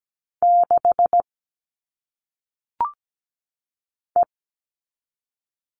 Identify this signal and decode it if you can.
Morse code, heard as a keyed tone, about 17 wpm, 700 Hz. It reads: 6 E
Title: All Cut Numbers - 1-Digit 17wpm